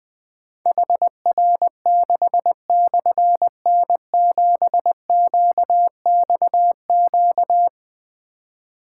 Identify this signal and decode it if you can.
Morse code, heard as a keyed tone, about 21 wpm, 700 Hz. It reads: HR6/N7QXQ
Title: Difficult Call Signs 20wpm